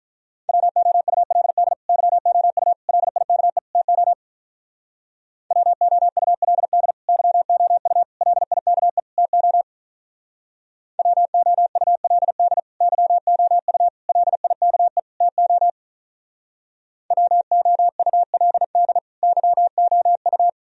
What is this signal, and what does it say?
Morse code, heard as a keyed tone, about 44 wpm, 700 Hz. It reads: WOULD YOU LIKE TO WOULD YOU LIKE TO WOULD YOU LIKE TO WOULD YOU